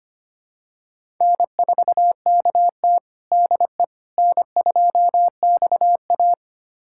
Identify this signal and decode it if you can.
Morse code, heard as a keyed tone, about 25 wpm, 700 Hz. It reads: N4KT DE N2XA
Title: USA Callsign DE Callsign - 4 Characters - 25wpm